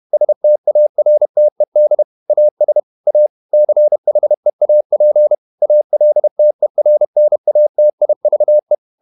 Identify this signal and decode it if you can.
Morse code, heard as a keyed tone, about 31 wpm, 600 Hz. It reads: STARTED AS A CHEAP ALTERNATIVE
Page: Never 31wpm | CW med Gnister